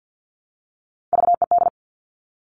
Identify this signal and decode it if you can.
Morse code, heard as a keyed tone, about 50 wpm, 700 Hz. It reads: VED
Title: Top Three Letter Combinations in Words 50wpm